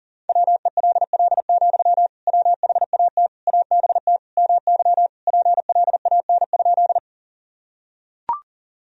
Morse code, heard as a keyed tone, about 40 wpm, 700 Hz. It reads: WELL, WHAT ABT MY PLAN?
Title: Sentences from Top 500 Words and Abbreviations 40wpm